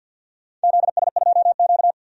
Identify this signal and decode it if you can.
Morse code, heard as a keyed tone, about 50 wpm, 700 Hz. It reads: ZS2Q
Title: DXCC-100 Call Signs - 4 Characters - 50wpm